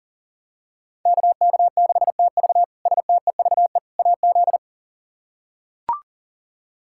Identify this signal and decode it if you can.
Morse code, heard as a keyed tone, about 40 wpm, 700 Hz. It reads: KK6TV STEVE AZ
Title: K1USN SST QSO Practice - 40wpm